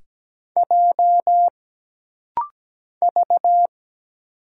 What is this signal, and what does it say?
Morse code, heard as a keyed tone, about 17 wpm, 700 Hz. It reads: J V
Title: Single Letters - Rapid-Fire 17wpm